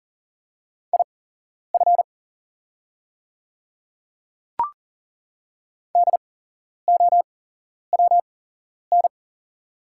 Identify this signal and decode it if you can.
Morse code, heard as a keyed tone, about 40 wpm, spaced as for 11 wpm, 700 Hz. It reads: IF DOWN
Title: 8x Character Spacing - Top 100 Words - 40wpm